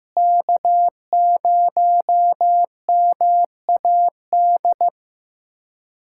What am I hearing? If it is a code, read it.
Morse code, heard as a keyed tone, about 15 wpm, 700 Hz. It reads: K0MAD
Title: USA Callsign DE Callsign - 5 Characters - 15wpm